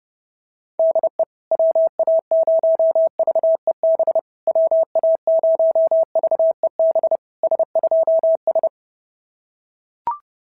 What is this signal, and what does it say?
Morse code, heard as a keyed tone, about 30 wpm, 650 Hz. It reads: DE WA0VEB WA0VEB S2S
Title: SOTA QSO Practice 30wpm